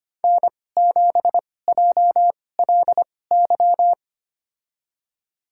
Morse code, heard as a keyed tone, about 25 wpm, spaced as for 18 wpm, 700 Hz. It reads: N7JLY
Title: USA Call Signs - 5 Characters - 2x Character Spacing - 25wpm